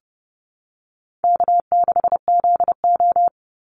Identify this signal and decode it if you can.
Morse code, heard as a keyed tone, about 30 wpm, 700 Hz. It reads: K6ZO
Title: Easy Call Signs Repeated 3x - 30wpm